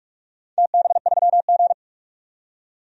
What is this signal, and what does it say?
Morse code, heard as a keyed tone, about 41 wpm, 700 Hz. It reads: TB3G
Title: DXCC-100 Call Signs - 4 Characters - 45wpm